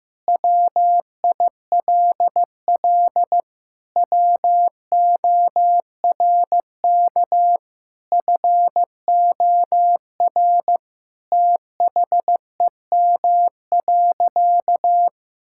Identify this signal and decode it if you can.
Morse code, heard as a keyed tone, about 15 wpm, 700 Hz. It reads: WILL WORK FOR THEM.
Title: Sentences from Top 1000 Words - Review Part 5 - 15wpm